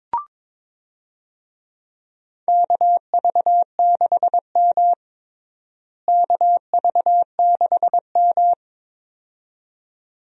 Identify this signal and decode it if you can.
Morse code, heard as a keyed tone, about 21 wpm, 700 Hz. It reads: KV6M KV6M
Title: USA Call Signs - 4 Characters - Repeated 2x - 22wpm